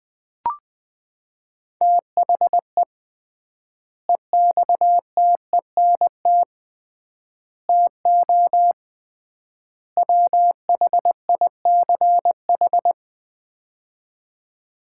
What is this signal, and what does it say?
Morse code, heard as a keyed tone, about 20 wpm, 700 Hz. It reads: THE EXTENT TO WHICH
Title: 3x Word-Spacing - Sets of 4 Words - 20wpm